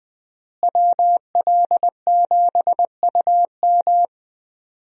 Morse code, heard as a keyed tone, about 20 wpm, 700 Hz. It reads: WL7UM
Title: USA Call Signs - 5 Characters - Repeated 3x - 20wpm